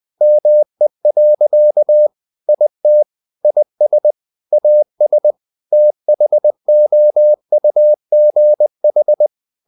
Morse code, heard as a keyed tone, about 20 wpm, 600 Hz. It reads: ME. IT IS AS THOUGH
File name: the old man and the sea / the old man and the sea part 1 20wpm